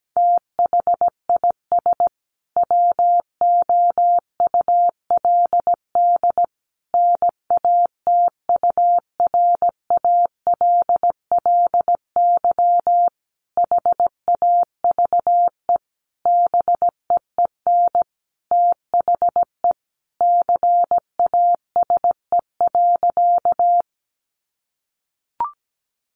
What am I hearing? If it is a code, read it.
Morse code, heard as a keyed tone, about 17 wpm, 700 Hz. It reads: THIS WOULD NATURALLY HAVE BEEN THE CASE.
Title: Sentences from Top 1300 Words 17wpm